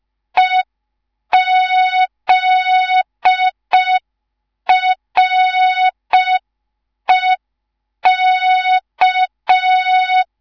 Morse code, heard as a keyed tone, about 5 wpm, 3700 Hz. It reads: EZREK